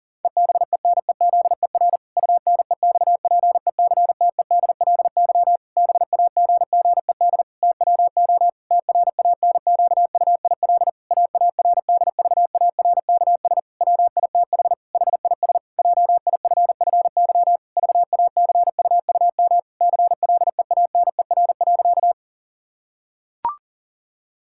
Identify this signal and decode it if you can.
Morse code, heard as a keyed tone, about 40 wpm, 700 Hz. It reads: EBENEZER UNEXPECTEDLY BAGGED TWO TRANQUIL AARDVARKS WITH HIS JIFFY VACUUM CLEANER.
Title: Pangram Challenge v2 - 40wpm